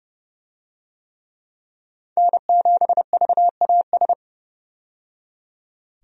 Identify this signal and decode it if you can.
Morse code, heard as a keyed tone, about 29 wpm, 700 Hz. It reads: N7VAS
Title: USA Call Signs - 5 Characters - 30wpm